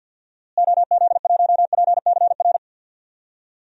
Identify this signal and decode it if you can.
Morse code, heard as a keyed tone, about 52 wpm, 700 Hz. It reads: OZ1PCR
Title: DXCC-100 Call Signs - 6 Characters - Repeated 2x - 50wpm